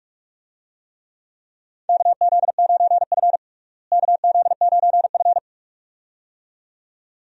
Morse code, heard as a keyed tone, about 42 wpm, 700 Hz. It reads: KZ9F KZ9F
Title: USA Call Signs - 4 Characters - Repeated 2x - 45wpm